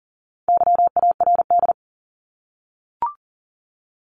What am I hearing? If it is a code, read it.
Morse code, heard as a keyed tone, about 40 wpm, 700 Hz. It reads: YARD
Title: Top 1751-2000 Words 40wpm